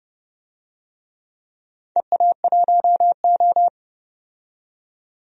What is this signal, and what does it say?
Morse code, heard as a keyed tone, about 30 wpm, 700 Hz. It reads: EA1O
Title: DXCC-100 Call Signs - 4 Characters - 30wpm